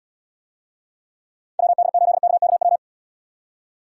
Morse code, heard as a keyed tone, about 78 wpm, 700 Hz. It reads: WD8KKW